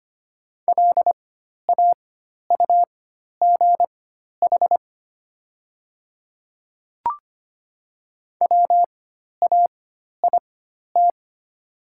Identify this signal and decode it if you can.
Morse code, heard as a keyed tone, about 25 wpm, spaced as for 12 wpm, 700 Hz. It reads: LAUGH WAIT
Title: 4x Character Spacing - Top 500 Words - 25wpm